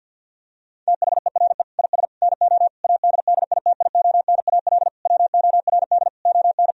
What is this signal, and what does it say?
Morse code, heard as a keyed tone, about 50 wpm, 700 Hz. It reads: THERE IS NO ADDITIONAL WORD ON
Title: Sentences from Top 1000 Words - Review Part 4 - 50wpm